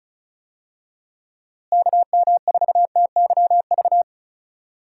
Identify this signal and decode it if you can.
Morse code, heard as a keyed tone, about 36 wpm, 700 Hz. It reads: KM4TYV
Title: USA Call Signs - 6 Characters - Repeated 3x - 35wpm